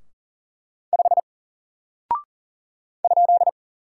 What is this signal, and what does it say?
Morse code, heard as a keyed tone, about 40 wpm, 700 Hz. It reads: 5 ?
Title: Single Character (Letter:Number:Punctuation:Prosign) Alt 40wpm